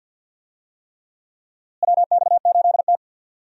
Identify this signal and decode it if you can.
Morse code, heard as a keyed tone, about 52 wpm, 700 Hz. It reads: WX8T
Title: USA Call Signs - 4 Characters - Repeated 3x - 50wpm